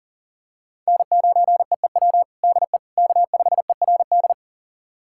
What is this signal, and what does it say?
Morse code, heard as a keyed tone, about 40 wpm, 700 Hz. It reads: N9EEW DE K5ERD
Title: USA Callsign DE Callsign - 5 Characters - 40wpm